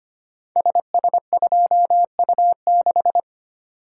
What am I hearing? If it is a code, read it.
Morse code, heard as a keyed tone, about 25 wpm, 700 Hz. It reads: SS2U6
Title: Code Groups - 5 Letter:Numbers - 25wpm